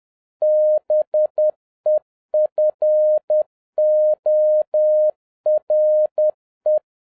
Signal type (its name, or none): Morse code